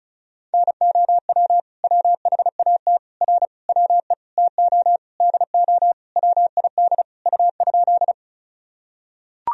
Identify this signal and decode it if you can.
Morse code, heard as a keyed tone, about 36 wpm, 700 Hz. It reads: NOW WHAT R WE TO DO WID U?
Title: Sentences from Top 100 Words and Abbreviations 35wpm